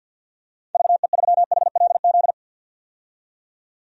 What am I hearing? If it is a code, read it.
Morse code, heard as a keyed tone, about 52 wpm, 700 Hz. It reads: VE3HLZ
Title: DXCC-100 Call Signs - 6 Characters - 50wpm